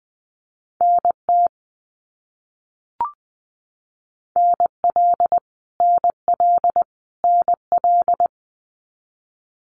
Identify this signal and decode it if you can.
Morse code, heard as a keyed tone, about 20 wpm, 700 Hz. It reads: NT NL NL NL
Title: Canadian Provinces and Territories - Repeated 3x - 20wpm